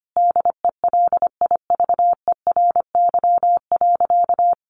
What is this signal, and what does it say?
Morse code, heard as a keyed tone, about 25 wpm, 700 Hz. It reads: DELIVERY.
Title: Sentences from Top 1500 Words 25wpm